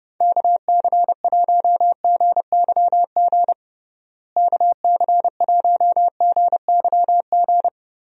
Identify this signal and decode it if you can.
Morse code, heard as a keyed tone, about 30 wpm, 700 Hz. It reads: KC1GYG KC1GYG
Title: USA Call Signs - 6 Characters - Repeated 2x - 30wpm